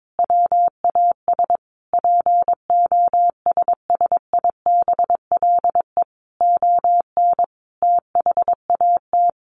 Morse code, heard as a keyed tone, about 22 wpm, 700 Hz. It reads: WAS POSSIBLE ON THAT
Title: Sentences from Top 1400 Words 22wpm